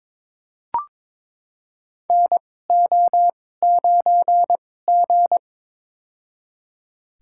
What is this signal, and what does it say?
Morse code, heard as a keyed tone, about 22 wpm, spaced as for 16 wpm, 700 Hz. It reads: NO9G
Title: USA Call Signs - 4 Characters - 2x Character Spacing - 22wpm